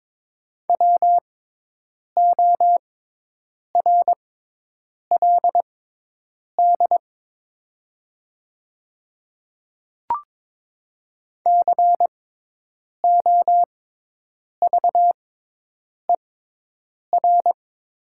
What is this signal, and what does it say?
Morse code, heard as a keyed tone, about 22 wpm, spaced as for 8 wpm, 700 Hz. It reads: WORLD COVER
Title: 6x Character Spacing - Top 500 Words - 22wpm